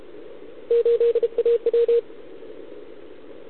Morse code, heard as a keyed tone, about 33 wpm, 470 Hz. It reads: 8AW